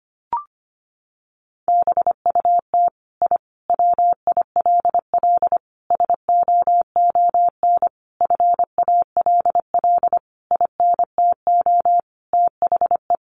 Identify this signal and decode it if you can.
Morse code, heard as a keyed tone, about 25 wpm, 700 Hz. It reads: BUT I WILL SOON FALL INTO THE
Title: Sentences from Top 1300 Words 25wpm